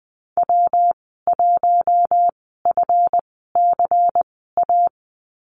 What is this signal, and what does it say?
Morse code, heard as a keyed tone, about 20 wpm, spaced as for 15 wpm, 700 Hz. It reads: W1FCA
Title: 173 Lesson 28g - Number 9 - Call Signs 20wpm